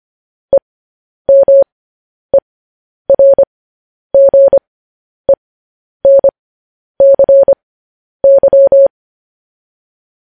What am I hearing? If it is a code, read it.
Morse code, heard as a keyed tone, about 25 wpm, spaced as for 10 wpm, 550 Hz. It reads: EMERGENCY
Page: news-headlines-in-morse-code-at-10-wpm